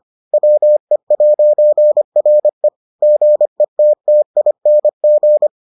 Morse code, heard as a keyed tone, about 25 wpm, 600 Hz. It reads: WE'RE GETTING